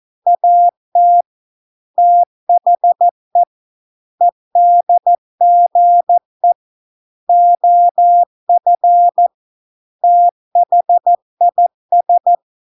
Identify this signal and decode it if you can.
Morse code, heard as a keyed tone, about 14 wpm, 700 Hz. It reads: AT THE EDGE OF THIS